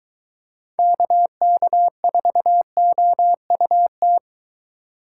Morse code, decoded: KK4OUT

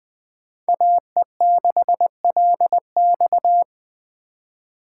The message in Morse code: AE6LX

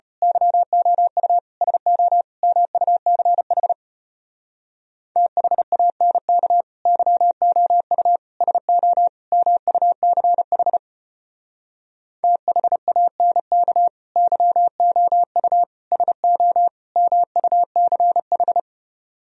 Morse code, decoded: YOU SO MUCH THANK YOU SO MUCH THANK YOU SO MUCH

